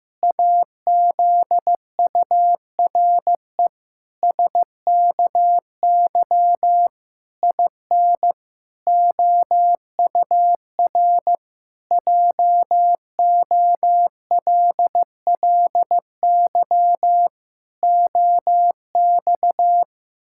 Morse code, heard as a keyed tone, about 16 wpm, 700 Hz. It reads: AZURE SKY IN OUR JOLLY OX
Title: Pangram Challenge v2 - 15wpm